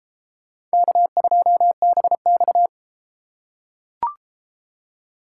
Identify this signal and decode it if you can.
Morse code, heard as a keyed tone, about 30 wpm, 700 Hz. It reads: K2BX